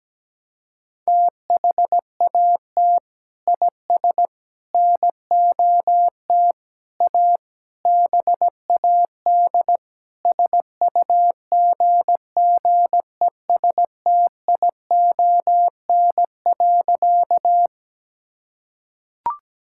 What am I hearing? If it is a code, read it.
Morse code, heard as a keyed tone, about 17 wpm, 700 Hz. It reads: THAT IS NOT A BAD SUGGESTION.
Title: Sentences from Top 1600 Words 17wpm